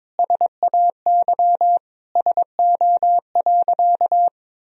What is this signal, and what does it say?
Morse code, heard as a keyed tone, about 22 wpm, 700 Hz. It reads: SAY SO.
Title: Sentences from Top 1000 Words - Review Part 1 - 22wpm